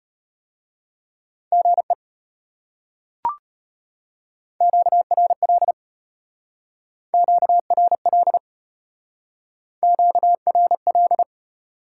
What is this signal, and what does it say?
Morse code, heard as a keyed tone, about 30 wpm, 700 Hz. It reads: GE QRL QRL QRL
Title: Speed-Racing Top 90 QSO Elements 25wpm